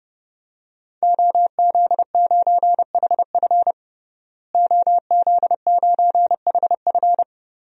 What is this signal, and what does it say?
Morse code, heard as a keyed tone, about 32 wpm, 700 Hz. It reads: OZ9HF OZ9HF
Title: DXCC-100 Call Signs - 5 Characters - Repeated 3x - 30wpm